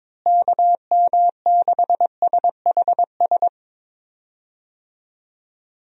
Morse code, heard as a keyed tone, about 22 wpm, 700 Hz. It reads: KM6SHS
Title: USA Call Signs - 6 Characters - Repeated 2x - 22wpm